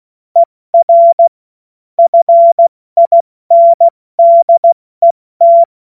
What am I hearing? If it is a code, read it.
Morse code, heard as a keyed tone, about 16 wpm, 680 Hz. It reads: ER FINDET
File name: Morsekurs / Witze / Witz 15. Juni 2021, 70 F80